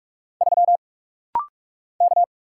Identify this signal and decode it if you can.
Morse code, heard as a keyed tone, about 46 wpm, 700 Hz. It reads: 3 K
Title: Single Letter-Number - Rapid-Fire 45wpm